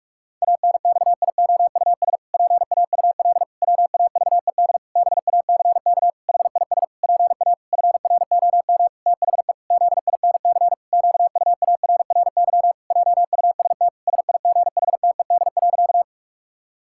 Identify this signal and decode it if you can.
Morse code, heard as a keyed tone, about 41 wpm, 700 Hz. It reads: ANXIOUS PAUL WAVED BACK HIS PA FROM THE ZINC QUARRY JUST SIGHTED.
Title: Pangram Challenge v2 - 45wpm